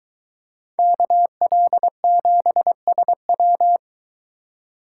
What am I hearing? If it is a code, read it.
Morse code, heard as a keyed tone, about 22 wpm, 700 Hz. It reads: KL7SW